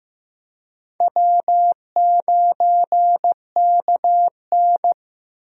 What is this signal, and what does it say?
Morse code, heard as a keyed tone, about 15 wpm, 700 Hz. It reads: W9KN
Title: USA Call Signs - 4 Characters - Repeated 2x - 15wpm